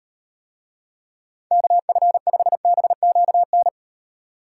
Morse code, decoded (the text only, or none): KF5BQN